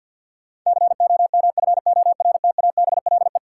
Morse code, heard as a keyed tone, about 47 wpm, 700 Hz. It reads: COMFORTABLE
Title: Sentences from Top 1500 Words - New Word Drill Alt - 50wpm